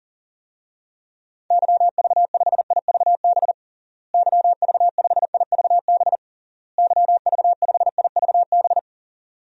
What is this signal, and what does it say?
Morse code, heard as a keyed tone, about 40 wpm, 700 Hz. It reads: YV5IVB YV5IVB YV5IVB
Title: DXCC-100 Call Signs - 6 Characters - Repeated 3x - 40wpm